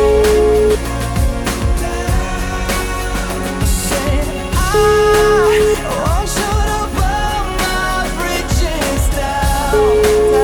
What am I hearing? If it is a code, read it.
Ringback tone (Europe, 425 Hz), with noise.